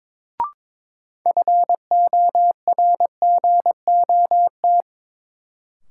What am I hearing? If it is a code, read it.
Morse code, heard as a keyed tone, about 23 wpm, 700 Hz. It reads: FORGOT
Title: Sentences from Top 1400 Words - New Word Drill - 22wpm